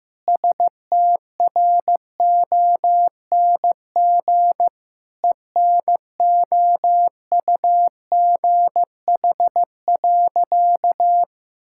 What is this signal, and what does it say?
Morse code, heard as a keyed tone, about 15 wpm, 700 Hz. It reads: STRONG ENOUGH.
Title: Sentences from Top 1600 Words 15wpm